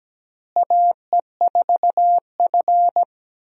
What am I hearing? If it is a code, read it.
Morse code, heard as a keyed tone, about 17 wpm, 700 Hz. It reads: AE4F
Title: USA Call Signs - 4 Characters - Repeated 3x - 17wpm